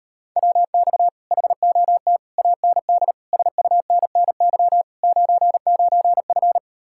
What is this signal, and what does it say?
Morse code, heard as a keyed tone, about 38 wpm, 700 Hz. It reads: WX HOT AND SUNNY 99F